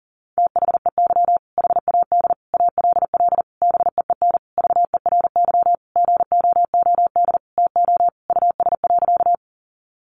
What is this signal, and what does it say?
Morse code, heard as a keyed tone, about 40 wpm, 700 Hz. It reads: THEY HAD ALL BEEN VERY GOOD TO US.
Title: Sentences from Top 200 Words 40wpm